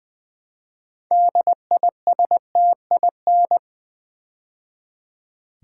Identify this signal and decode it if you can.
Morse code, heard as a keyed tone, about 20 wpm, 700 Hz. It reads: DISTIN
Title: Build A Word - 3 Letters At A Time - 8-Letter Words - 20wpm